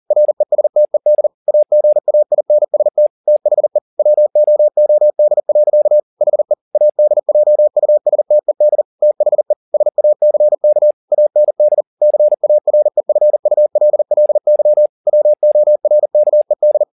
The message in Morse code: RESTED AGAINST THE WOOD. HE ADJUSTED THE SACK AND CAREFULLY WORKED

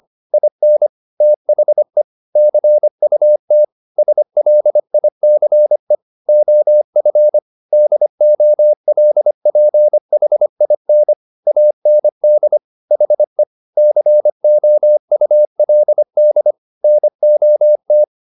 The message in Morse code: IN THE CUT SLICE OF DOLPHIN AND HE COULD NOT

